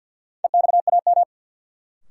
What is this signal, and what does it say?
Morse code, heard as a keyed tone, about 51 wpm, 700 Hz. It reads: EXAM
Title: Sentences from Top 1000 Words - Review Part 5 - New Word Drill - 50wpm